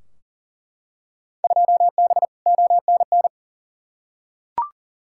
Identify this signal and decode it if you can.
Morse code, heard as a keyed tone, about 40 wpm, 700 Hz. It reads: 2B ONN
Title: Field Day Exchanges - 40wpm